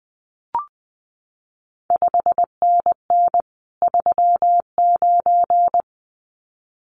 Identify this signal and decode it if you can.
Morse code, heard as a keyed tone, about 20 wpm, 700 Hz. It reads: 5NN 39
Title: CQ WW DX Contest Exchange - 20wpm